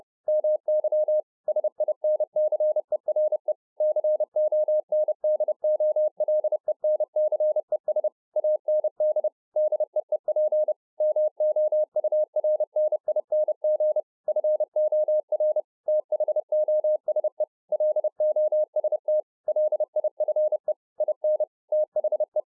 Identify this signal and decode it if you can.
Morse code, heard as a keyed tone, about 30 wpm, 610 Hz. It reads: MY SINCERE CONDOLENCES AND DEEP MOURNING FOR THOSE LOST LIFE IN THE